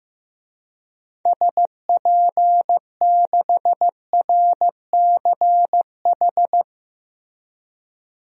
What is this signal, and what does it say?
Morse code, heard as a keyed tone, about 16 wpm, 700 Hz. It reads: SP6RCH
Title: DXCC-100 Call Signs - 6 Characters - Repeated 3x - 15wpm